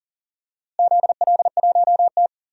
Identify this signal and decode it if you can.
Morse code, heard as a keyed tone, about 40 wpm, 700 Hz. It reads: ZL1T
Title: DXCC-100 Call Signs - 4 Characters - Repeated 2x - 40wpm